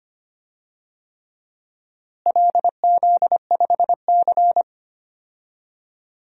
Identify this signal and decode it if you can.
Morse code, heard as a keyed tone, about 25 wpm, 700 Hz. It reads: LZ5C
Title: DXCC-100 Call Signs - 4 Characters - Repeated 3x - 25wpm